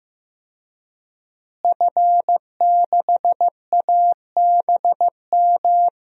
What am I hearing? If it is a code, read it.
Morse code, heard as a keyed tone, about 15 wpm, 700 Hz. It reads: F6ABM
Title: DXCC-100 Call Signs - 5 Characters - Repeated 3x - 15wpm